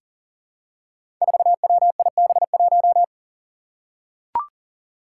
Morse code, heard as a keyed tone, about 40 wpm, 700 Hz. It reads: 4WIB1
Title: Five Challenging Letter-Number Combinations 40wpm